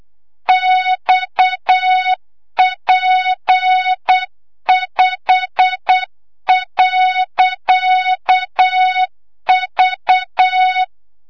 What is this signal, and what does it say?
Morse code, heard as a keyed tone, about 8 wpm, 3700 Hz, with noise. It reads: XP5.V
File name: Morze-HA2MN / hang / 25 GYAKORLAT